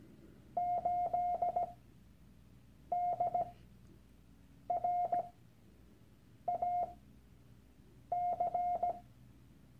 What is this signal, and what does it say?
Morse code, heard as a keyed tone, about 17 wpm, spaced as for 6 wpm, 710 Hz. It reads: 8DRAC
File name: Workshop Dateien / CW-Online-Kurs / Lektion 001